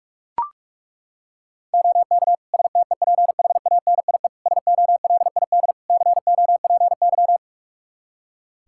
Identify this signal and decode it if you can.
Morse code, heard as a keyed tone, about 42 wpm, 700 Hz. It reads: OK STEPHANIE SOLID COPY
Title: Practice QSO - Exchange 3 - More Info - 45wpm